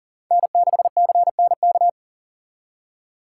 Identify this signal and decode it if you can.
Morse code, heard as a keyed tone, about 51 wpm, 700 Hz. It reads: N6CNK